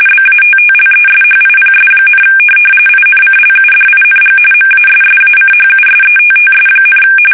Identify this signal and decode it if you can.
RTTY, about 45.45 baud. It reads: BATTLESHIPS, INCLUDING MASSACHUSETTS,
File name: fsk850 / USS-Massachusetts